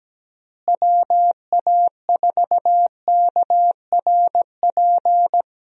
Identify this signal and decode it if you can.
Morse code, heard as a keyed tone, about 17 wpm, 700 Hz. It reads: WA4KRP
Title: USA Call Signs - 6 Characters - Repeated 3x - 17wpm